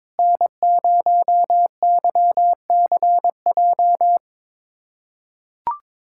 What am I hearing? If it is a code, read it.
Morse code, heard as a keyed tone, about 22 wpm, 700 Hz. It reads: N0YCJ
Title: USA Callsign DE Callsign - 5 Characters - 22wpm